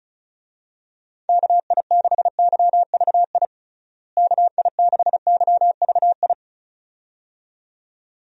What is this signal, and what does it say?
Morse code, heard as a keyed tone, about 36 wpm, 700 Hz. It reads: KI6YVI KI6YVI
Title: USA Call Signs - 6 Characters - Repeated 2x - 35wpm